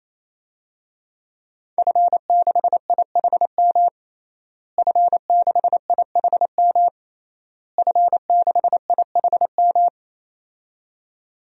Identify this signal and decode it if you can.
Morse code, heard as a keyed tone, about 28 wpm, 700 Hz. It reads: F6IHM F6IHM F6IHM
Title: DXCC-100 Call Signs - 5 Characters - Repeated 3x - 28wpm